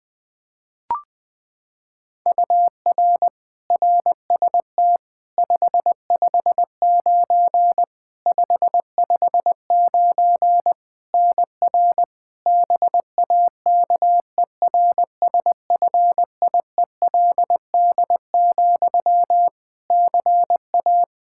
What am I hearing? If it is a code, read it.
Morse code, heard as a keyed tone, about 20 wpm, 700 Hz. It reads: UR RST 559 559 NR BAKERSFIELD, CA
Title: Practice QSO - Exchange 1 - Beginning - 20wpm